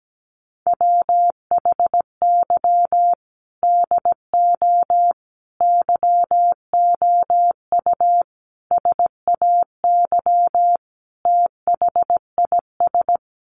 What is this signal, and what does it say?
Morse code, heard as a keyed tone, about 17 wpm, 700 Hz. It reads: WHY DO YOU SAY THIS
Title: Sentences from Top 1100 Words 17wpm